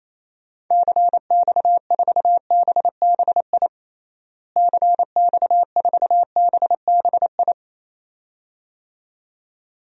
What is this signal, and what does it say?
Morse code, heard as a keyed tone, about 28 wpm, 700 Hz. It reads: CX4BBI CX4BBI
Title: DXCC-100 Call Signs - 6 Characters - Repeated 2x - 28wpm